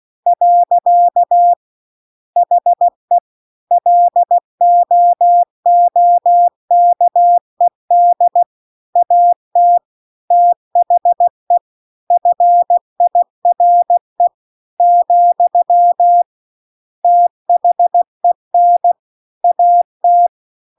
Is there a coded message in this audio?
Yes: Morse code from 0.3 to 20.3 s, heard as a keyed tone, about 16 wpm, 700 Hz. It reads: . HE LOOKED AT THE FIRE , THEN AT